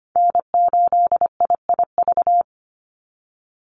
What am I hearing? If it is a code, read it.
Morse code, heard as a keyed tone, about 25 wpm, 700 Hz. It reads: N8IIV
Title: USA Call Signs 25wpm